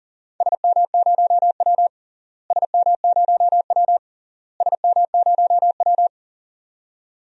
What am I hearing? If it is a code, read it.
Morse code, heard as a keyed tone, about 40 wpm, 700 Hz. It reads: SM0W SM0W SM0W
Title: DXCC-100 Call Signs - 4 Characters - Repeated 3x - 40wpm